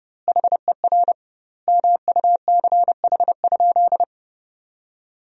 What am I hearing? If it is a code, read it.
Morse code, heard as a keyed tone, about 30 wpm, 700 Hz. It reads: HER MUCH?
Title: 2x Word-Spacing - Sentences from Top 400 Words - 30wpm